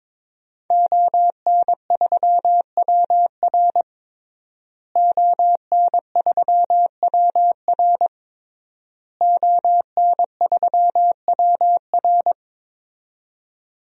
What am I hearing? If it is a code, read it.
Morse code, heard as a keyed tone, about 22 wpm, 700 Hz. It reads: ON3WR ON3WR ON3WR
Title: DXCC-100 Call Signs - 5 Characters - Repeated 3x - 22wpm